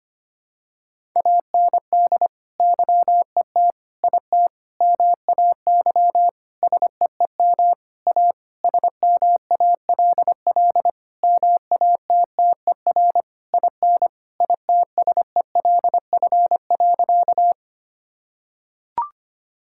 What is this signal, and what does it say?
Morse code, heard as a keyed tone, about 25 wpm, 700 Hz. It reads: AND YET IT MAY SEEM A SMALL MATTER IN ITSELF.
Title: Sentences from Top 1000 Words - Review Part 5 - 25wpm